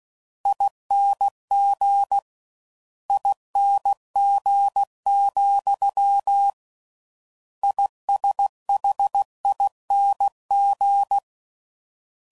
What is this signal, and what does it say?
Morse code, heard as a keyed tone, about 16 wpm, 800 Hz. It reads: ING ING, ISHING